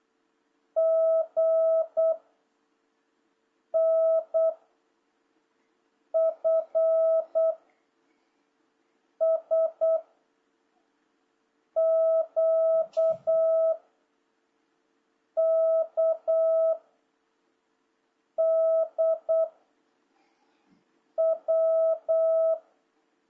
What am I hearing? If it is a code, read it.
Morse code, heard as a keyed tone, about 8 wpm, 660 Hz. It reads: G N F S Q K D W
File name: jh8cbh / 2026cw